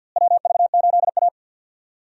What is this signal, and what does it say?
Morse code, heard as a keyed tone, about 46 wpm, 700 Hz. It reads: WV8A
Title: USA Call Signs - 4 Characters - Repeated 3x - 50wpm